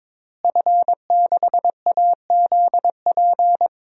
Morse code, heard as a keyed tone, about 22 wpm, 700 Hz. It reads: F6AZP
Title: DXCC-100 Call Signs - 5 Characters - Repeated 2x - 22wpm